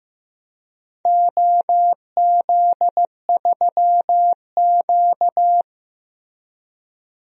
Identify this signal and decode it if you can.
Morse code, heard as a keyed tone, about 15 wpm, 700 Hz. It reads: OZ3Q
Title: DXCC-100 Call Signs - 4 Characters - Repeated 3x - 15wpm